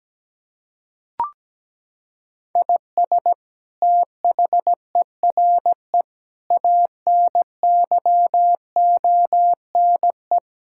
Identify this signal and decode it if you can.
Morse code, heard as a keyed tone, about 17 wpm, 700 Hz. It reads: IS THERE ANYONE
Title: Sentences from Top 1000 Words - Review Part 6 - 17wpm